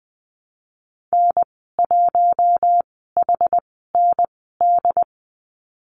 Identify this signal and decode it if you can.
Morse code, heard as a keyed tone, about 20 wpm, spaced as for 15 wpm, 700 Hz. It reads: N1HND
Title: 088 Lesson 15g - Number 5 - Call Signs 20wpm